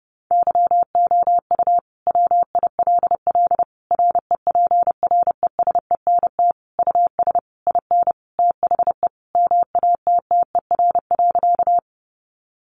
Morse code, heard as a keyed tone, about 30 wpm, 700 Hz. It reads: YOU WILL REPRESENT US IN THE MATTER.
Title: Sentences from Top 1200 Words 30wpm